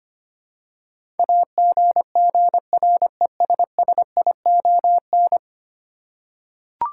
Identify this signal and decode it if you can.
Morse code, heard as a keyed tone, about 25 wpm, 700 Hz. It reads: AGGRESSION
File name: 10 Letter Words - First 3 Letters Given - 25wpm